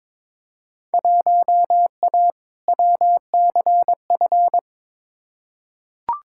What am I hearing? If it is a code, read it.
Morse code, heard as a keyed tone, about 22 wpm, 700 Hz. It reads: 1A WCF
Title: Field Day Exchanges - 22wpm